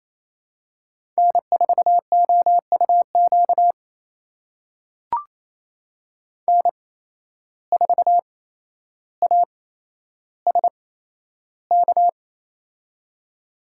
Morse code, heard as a keyed tone, about 28 wpm, spaced as for 12 wpm, 700 Hz. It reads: N4OUQ N4ASK
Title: USA Call Signs - 5 Characters - 8x Character Spacing - 28wpm